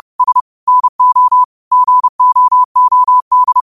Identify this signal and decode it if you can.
Morse code, heard as a keyed tone, about 30 wpm, 1000 Hz. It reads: S NO GOOD
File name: qotdfiles / 30wpm